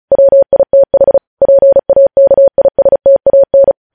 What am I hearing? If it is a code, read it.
Morse code, heard as a keyed tone, about 36 wpm, 550 Hz. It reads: WITH PAKISTAN
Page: news headlines in morse code at 35 wpm